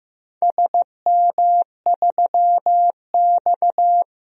Morse code, heard as a keyed tone, about 16 wpm, 700 Hz. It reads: SM3X
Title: DXCC-100 Call Signs - 4 Characters - 15wpm